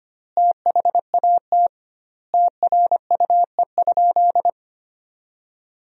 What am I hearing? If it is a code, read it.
Morse code, heard as a keyed tone, about 25 wpm, 700 Hz. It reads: THAT TRUE?
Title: 2x Word-Spacing - Sentences from Top 400 Words - 25wpm